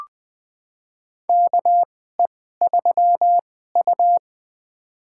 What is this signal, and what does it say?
Morse code, heard as a keyed tone, about 21 wpm, spaced as for 15 wpm, 700 Hz. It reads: KE3U
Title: USA Call Signs - 4 Characters - 2x Character Spacing - 20wpm